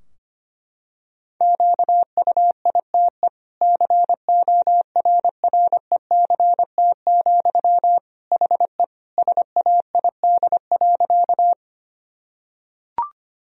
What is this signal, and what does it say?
Morse code, heard as a keyed tone, about 25 wpm, 700 Hz. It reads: QUITE CORRECT, HE SAID.
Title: Sentences from Top 1000 Words - Review Part 2 - 25wpm